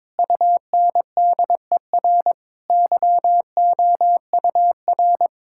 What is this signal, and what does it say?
Morse code, heard as a keyed tone, about 22 wpm, 700 Hz. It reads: UNDER YOUR
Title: Sentences from Top 1600 Words 22wpm